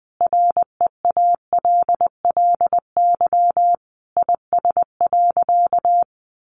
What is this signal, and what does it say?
Morse code, heard as a keyed tone, about 20 wpm, 700 Hz. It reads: REALLY IS.
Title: Sentences from Top 600 Words 20wpm